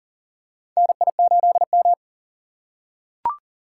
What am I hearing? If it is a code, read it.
Morse code, heard as a keyed tone, about 40 wpm, 700 Hz. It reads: NI8M
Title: USA Call Signs - 4 Characters - 2x Character Spacing - 40wpm